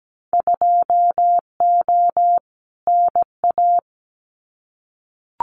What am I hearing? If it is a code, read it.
Morse code, heard as a keyed tone, about 17 wpm, 700 Hz. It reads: 2O NA
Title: Winter Field Day Exchanges 17wpm